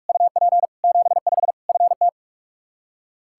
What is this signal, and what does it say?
Morse code, heard as a keyed tone, about 46 wpm, 700 Hz. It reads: UP 75 FT
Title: Practice QSO - Exchange 2 - Info - 45wpm